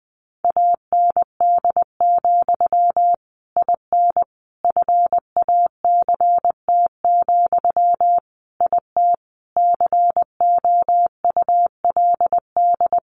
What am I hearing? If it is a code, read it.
Morse code, heard as a keyed tone, about 20 wpm, 700 Hz. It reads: AND, IN FACT, IT COULD
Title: Sentences from Top 1200 Words 20wpm